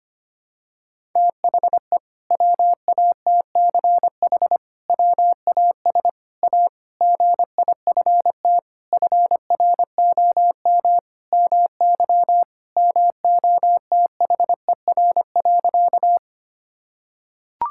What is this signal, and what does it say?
Morse code, heard as a keyed tone, about 25 wpm, 700 Hz. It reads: THE WATCH WAS A GIFT FROM MY MOTHER.
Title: Sentences from Top 1000 Words - Review Part 6 - 25wpm